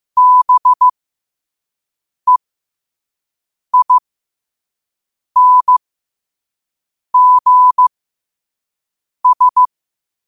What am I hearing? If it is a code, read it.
Morse code, heard as a keyed tone, about 15 wpm, spaced as for 5 wpm, 1000 Hz. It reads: BEINGS
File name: qotdfiles / 5wpm